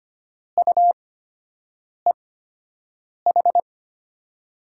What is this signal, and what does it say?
Morse code, heard as a keyed tone, about 25 wpm, spaced as for 7 wpm, 700 Hz. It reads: UEH
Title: USA Call Signs - 6 Characters - 8x Character Spacing - 25wpm